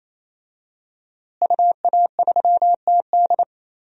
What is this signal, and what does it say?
Morse code, heard as a keyed tone, about 27 wpm, 700 Hz. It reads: UA3TD
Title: DXCC-100 Call Signs - 5 Characters - Repeated 3x - 28wpm